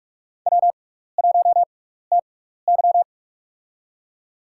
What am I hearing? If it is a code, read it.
Morse code, heard as a keyed tone, about 47 wpm, spaced as for 16 wpm, 700 Hz. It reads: W1TY
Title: USA Call Signs - 4 Characters - 6x Character Spacing - 45wpm